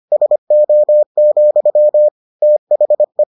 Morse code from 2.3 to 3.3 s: THE

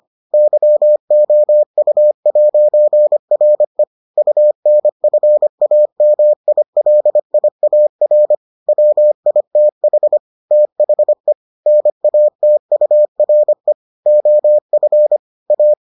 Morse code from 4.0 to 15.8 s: UNFAMILIAR WITH THE NATURE OF A

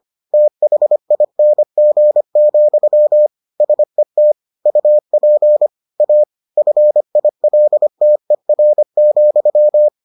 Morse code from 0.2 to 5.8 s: THING, SET UP

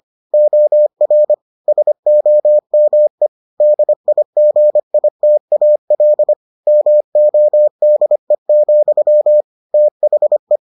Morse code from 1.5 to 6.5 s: SOME DIGITAL

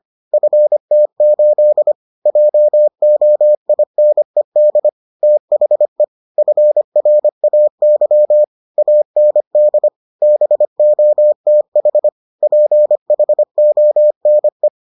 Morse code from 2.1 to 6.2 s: JOINED THE

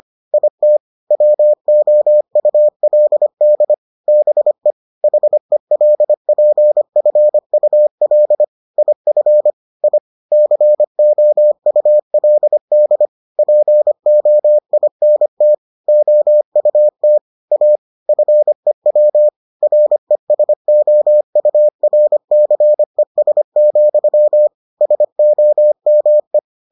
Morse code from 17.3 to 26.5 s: A FEW RESOURCES, SOME

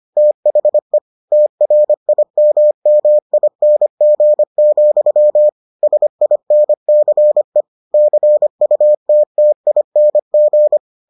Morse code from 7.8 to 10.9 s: CUTTING

